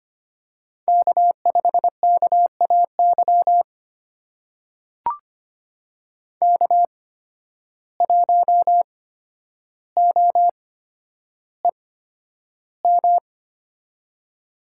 Morse code, heard as a keyed tone, about 25 wpm, spaced as for 11 wpm, 700 Hz. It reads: K5KAY K1OEM
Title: USA Call Signs - 5 Characters - 8x Character Spacing - 25wpm